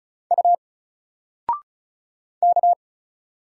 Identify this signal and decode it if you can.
Morse code, heard as a keyed tone, about 34 wpm, 700 Hz. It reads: U K
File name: Single-Letter-Number-35wpm